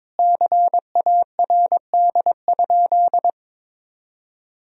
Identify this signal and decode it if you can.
Morse code, heard as a keyed tone, about 22 wpm, 700 Hz. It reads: CARD?
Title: Sentences from Top 1000 Words - Review Part 4 - 22wpm